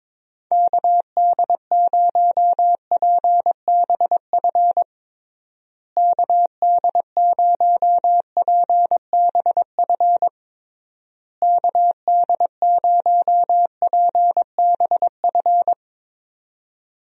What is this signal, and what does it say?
Morse code, heard as a keyed tone, about 22 wpm, 700 Hz. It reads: KD0PBF KD0PBF KD0PBF
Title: USA Call Signs - 6 Characters - Repeated 3x - 22wpm